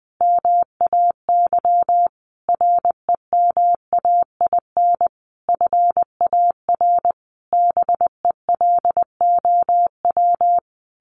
Morse code, heard as a keyed tone, about 19 wpm, 700 Hz. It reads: MAY REMAIN FAR BELOW
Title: Sentences from Top 1400 Words 20wpm